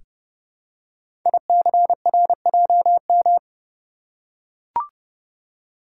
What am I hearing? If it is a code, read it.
Morse code, heard as a keyed tone, about 30 wpm, 700 Hz. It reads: ICRJM
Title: Code Groups - 5 Letters - 30wpm